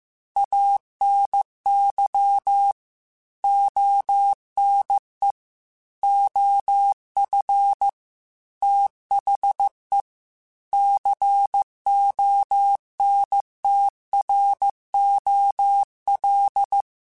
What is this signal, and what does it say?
Morse code, heard as a keyed tone, about 15 wpm, 800 Hz. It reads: ANY ONE OF THE CONTROL
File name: Technical Subjects / Technical Subjects-mp3s / OR Gate @13